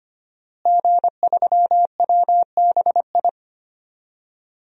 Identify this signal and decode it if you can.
Morse code, heard as a keyed tone, about 25 wpm, 700 Hz. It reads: G3WBI
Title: DXCC-100 Call Signs - 5 Characters - Repeated 3x - 25wpm